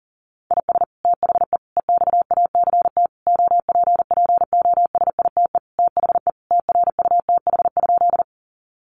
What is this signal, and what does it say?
Morse code, heard as a keyed tone, about 40 wpm, 700 Hz. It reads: IS THE EXACT OPPOSITE THE TRUTH?
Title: Sentences from Top 1300 Words 40wpm